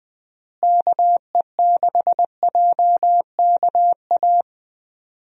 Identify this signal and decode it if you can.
Morse code, heard as a keyed tone, about 21 wpm, 700 Hz. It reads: KE6JKA
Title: USA Call Signs - 6 Characters - Repeated 2x - 20wpm